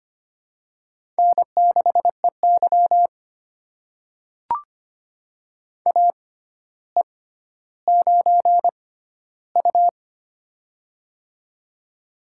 Morse code, heard as a keyed tone, about 25 wpm, spaced as for 13 wpm, 700 Hz. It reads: N6EY AE9U
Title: USA Call Signs - 4 Characters - 6x Character Spacing - 25wpm